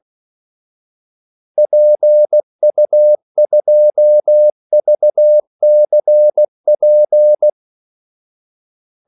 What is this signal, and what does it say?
Morse code, heard as a keyed tone, about 16 wpm, 600 Hz. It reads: PU2VCP
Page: Kaldesignaler 16 wpm